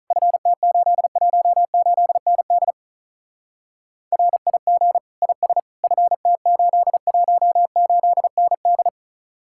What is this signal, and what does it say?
Morse code, heard as a keyed tone, about 44 wpm, 700 Hz. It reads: FT818ND RIG IS FT818ND